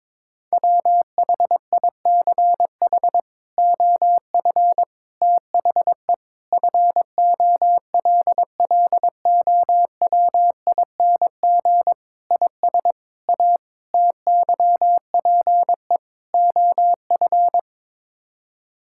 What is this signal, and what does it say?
Morse code, heard as a keyed tone, about 22 wpm, 700 Hz. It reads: WHICH OF THE FOLLOWING IS A TYPE OF
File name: practice / 22wpm / Sets of 8 Words - 22wpm